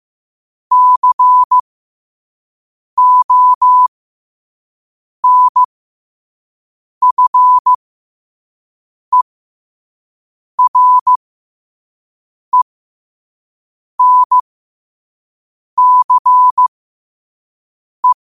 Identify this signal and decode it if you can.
Morse code, heard as a keyed tone, about 15 wpm, spaced as for 5 wpm, 1000 Hz. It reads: CONFERENCE
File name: qotdfiles / 5wpm